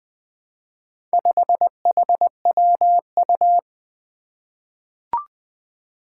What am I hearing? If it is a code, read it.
Morse code, heard as a keyed tone, about 20 wpm, 700 Hz. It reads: 5HWU
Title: Four Challenging Letter-Number Combinations 20wpm